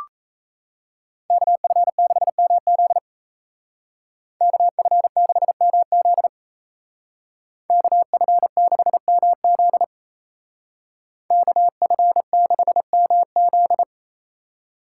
Morse code, heard as a keyed tone, about 34 wpm, 700 Hz. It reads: KF6MZ KF6MZ KF6MZ KF6MZ
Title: Speed Racing - USA 2x2 Call Signs 28wpm